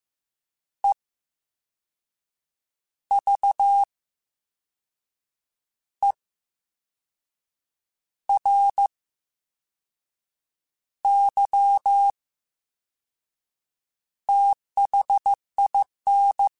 Morse code, heard as a keyed tone, about 15 wpm, spaced as for 5 wpm, 800 Hz. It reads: EVERYTHIN
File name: Special INGs / Special INGs-mp3s / Slow ing @13